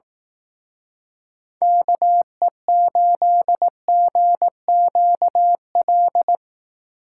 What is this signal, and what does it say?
Morse code, heard as a keyed tone, about 18 wpm, 700 Hz. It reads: KE8GQL
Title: Speed Racing - USA Call Signs 15wpm